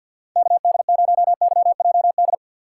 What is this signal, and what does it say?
Morse code, heard as a keyed tone, about 47 wpm, 700 Hz. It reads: KD0YJD
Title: USA Call Signs - 6 Characters - 3x Character Spacing - 50wpm